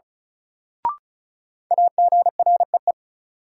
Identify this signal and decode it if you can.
Morse code, heard as a keyed tone, about 36 wpm, 700 Hz. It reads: AGREE
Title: Sentences from Top 1000 Words - Review Part 6 - New Word Drill - 35wpm